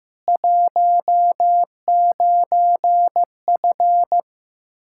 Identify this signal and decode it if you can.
Morse code, heard as a keyed tone, about 15 wpm, 700 Hz. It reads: 19F
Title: Practice QSO - Exchange 2 - Info - 15wpm